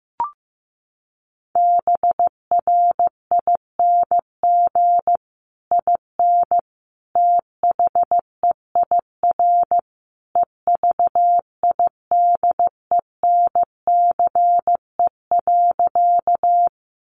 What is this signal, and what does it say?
Morse code, heard as a keyed tone, about 15 wpm, 700 Hz. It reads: BRING IN THEIR EVIDENCE.
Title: Sentences from Top 1300 Words 15wpm